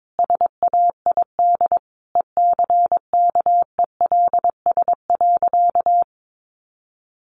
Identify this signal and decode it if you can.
Morse code, heard as a keyed tone, about 22 wpm, 700 Hz. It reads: SAID ECKELS.
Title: A Sound Of Thunder 22wpm